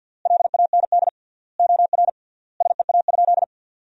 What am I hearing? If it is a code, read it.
Morse code, heard as a keyed tone, about 50 wpm, 700 Hz. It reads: LAND OR SEA?
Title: 3x Word-Spacing - Sentences from Top 500 Words - 50wpm